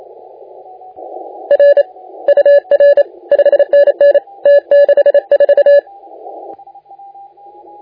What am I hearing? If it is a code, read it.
Morse code, heard as a keyed tone, about 31 wpm, 570 Hz. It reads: R UR 5NN T64
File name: la4c / sac-cw-2018 / CONTEST AUDIO 1600Z